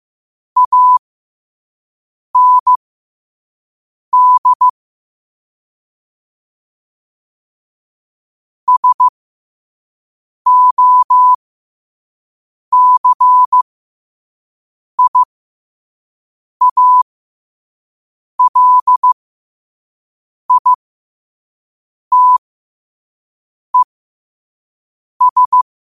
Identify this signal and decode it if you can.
Morse code, heard as a keyed tone, about 15 wpm, spaced as for 5 wpm, 1000 Hz. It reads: AND SOCIALITES
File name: qotdfiles / 5wpm